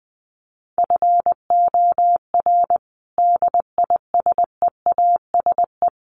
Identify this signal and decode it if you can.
Morse code, heard as a keyed tone, about 20 wpm, 700 Hz. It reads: FOR DISEASE
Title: Sets of 5 Words Encore2 20wpm